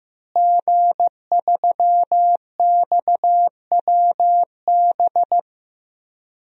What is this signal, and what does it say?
Morse code, heard as a keyed tone, about 15 wpm, 700 Hz. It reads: G3XWB
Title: DXCC-100 Call Signs - 5 Characters - Repeated 3x - 15wpm